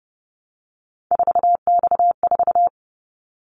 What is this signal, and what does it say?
Morse code, heard as a keyed tone, about 30 wpm, 700 Hz. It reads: 4X4
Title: Callsign Prefixes for Top 50 DXCC - 30wpm